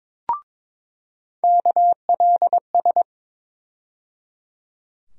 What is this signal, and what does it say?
Morse code, heard as a keyed tone, about 22 wpm, 700 Hz. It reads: KLS
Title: USA Call Sign Postfixes - 3 Letters 22wpm